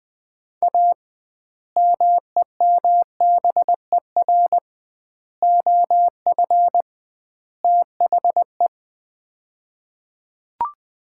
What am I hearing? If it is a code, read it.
Morse code, heard as a keyed tone, about 20 wpm, 700 Hz. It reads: A MEMBER OF THE
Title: 2x Word-Spacing - Sets of 4 Words - 20wpm